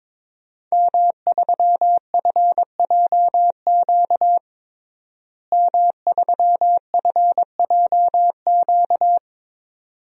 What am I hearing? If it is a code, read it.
Morse code, heard as a keyed tone, about 21 wpm, 700 Hz. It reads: M3FJQ M3FJQ
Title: DXCC-100 Call Signs - 5 Characters - Repeated 2x - 22wpm